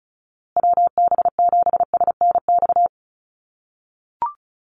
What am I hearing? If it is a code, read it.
Morse code, heard as a keyed tone, about 34 wpm, 700 Hz. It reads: WB7SNX
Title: USA Call Signs Repeated 3x 35wpm